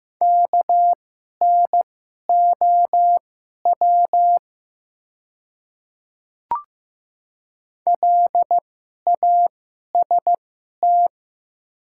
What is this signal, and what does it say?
Morse code, heard as a keyed tone, about 15 wpm, spaced as for 11 wpm, 700 Hz. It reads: KNOW LAST
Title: 2x Character Spacing - Top 300 Words - 15wpm